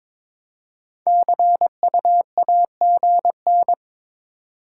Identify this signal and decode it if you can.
Morse code, heard as a keyed tone, about 22 wpm, 700 Hz. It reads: CUAGN
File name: Top 20 QSO Elements - 4x Character Spacing - 22wpm